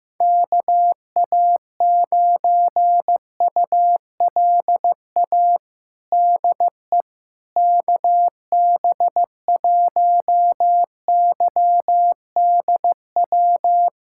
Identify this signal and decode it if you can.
Morse code, heard as a keyed tone, about 15 wpm, 700 Hz. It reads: KA9ULA DE KB1YDW
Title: USA Callsign DE Callsign - 6 Characters - 15wpm